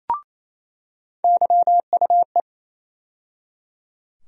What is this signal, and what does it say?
Morse code, heard as a keyed tone, about 28 wpm, 700 Hz. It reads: YUE
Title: USA Call Sign Postfixes - 3 Letters 28wpm